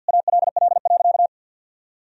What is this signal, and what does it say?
Morse code, heard as a keyed tone, about 50 wpm, 700 Hz. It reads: ALL.
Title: Sentences from Top 1000 Words - Review Part 4 - 50wpm